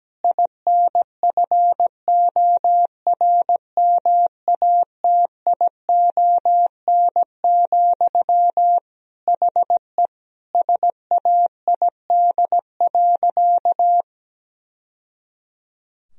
Morse code, heard as a keyed tone, about 17 wpm, 700 Hz. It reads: INFORMATION, HE SAID.
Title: Sentences from Top 1000 Words - Review Part 3 - 17wpm